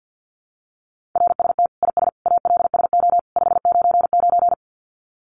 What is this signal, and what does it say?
Morse code, heard as a keyed tone, about 49 wpm, 700 Hz. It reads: RST IS ALSO 599
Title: Common QSO Phrases 50wpm